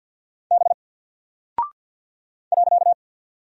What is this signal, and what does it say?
Morse code, heard as a keyed tone, about 49 wpm, 700 Hz. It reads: B .
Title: Single Character (Letter:Number:Punctuation:Prosign) 50wpm